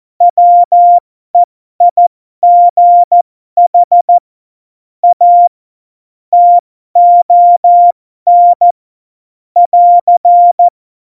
Morse code, heard as a keyed tone, about 14 wpm, spaced as for 12 wpm, 700 Hz. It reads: WEIGH A TON <AR>
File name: Morsekurs / Witze